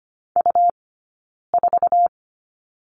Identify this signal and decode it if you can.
Morse code, heard as a keyed tone, about 25 wpm, 700 Hz. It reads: U 4
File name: Lesson 17d - Letter u - Two Characters 25wpm